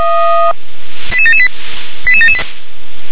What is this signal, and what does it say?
Five-tone selective call (ZVEI), heard as quick runs of five tones, tones of about 70 ms. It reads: 88778 88778
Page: ZVEI